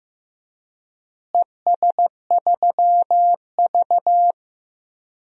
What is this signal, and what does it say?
Morse code, heard as a keyed tone, about 15 wpm, 700 Hz. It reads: ES3V
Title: DXCC-100 Call Signs - 4 Characters - Repeated 2x - 15wpm